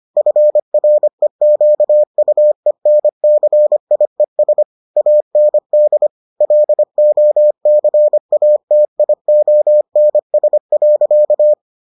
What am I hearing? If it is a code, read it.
Morse code, heard as a keyed tone, about 25 wpm, 600 Hz. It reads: FREQUENCIES AND LOCATIONS.